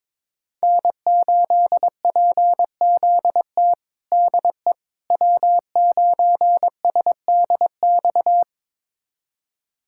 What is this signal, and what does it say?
Morse code, heard as a keyed tone, about 22 wpm, 700 Hz. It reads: N8PZT DE W9SDX
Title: USA Callsign DE Callsign - 5 Characters - 22wpm